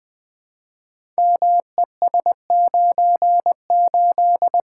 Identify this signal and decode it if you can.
Morse code, heard as a keyed tone, about 20 wpm, 700 Hz. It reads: MES98
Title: Code Groups - 5 Letter:Numbers - 20wpm